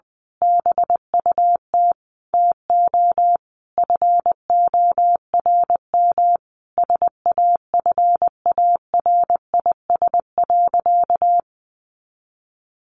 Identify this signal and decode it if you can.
Morse code, heard as a keyed tone, about 20 wpm, 700 Hz. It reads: BUT TO FORM SAFARIS.
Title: A Sound Of Thunder 20wpm